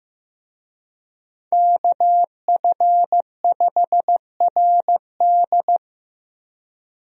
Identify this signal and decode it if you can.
Morse code, heard as a keyed tone, about 15 wpm, 700 Hz. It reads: KF5RD
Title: USA Call Signs - 5 Characters - Repeated 2x - 15wpm